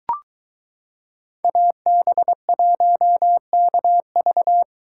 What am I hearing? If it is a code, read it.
Morse code, heard as a keyed tone, about 22 wpm, 700 Hz. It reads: AB1KV